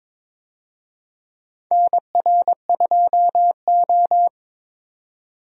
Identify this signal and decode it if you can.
Morse code, heard as a keyed tone, about 22 wpm, 700 Hz. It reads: NR2O
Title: USA Call Signs - 4 Characters - 22wpm